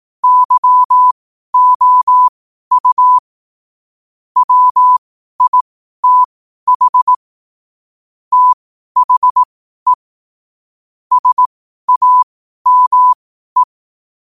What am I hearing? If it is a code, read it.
Morse code, heard as a keyed tone, about 18 wpm, spaced as for 12 wpm, 1000 Hz. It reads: YOU WITH THE SAME
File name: qotdfiles / 12wpm